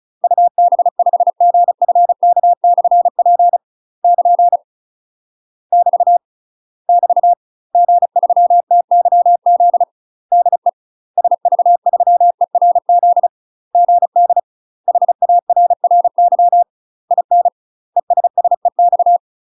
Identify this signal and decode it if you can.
Morse code, heard as a keyed tone, about 36 wpm, 700 Hz. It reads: UB5GFK/P <KN> <BT> <BT> G3TYZ DE SV3ERZ GD HARRY IN ESSEX
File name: UK MASTER QSOs TRAINING / 35WPM / UK MASTER QSOs 0041